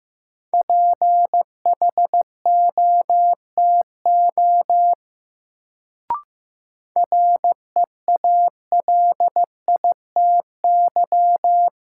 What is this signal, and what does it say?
Morse code, heard as a keyed tone, about 15 wpm, 700 Hz. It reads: PHOTO REALITY
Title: Sentences from Top 1200 Words - New Word Drill - 15wpm